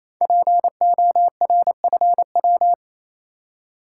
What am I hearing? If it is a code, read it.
Morse code, heard as a keyed tone, about 27 wpm, 700 Hz. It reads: PORFW
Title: Code Groups - 5 Letters - 28wpm